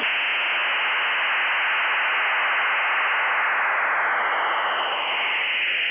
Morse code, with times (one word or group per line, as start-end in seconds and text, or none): none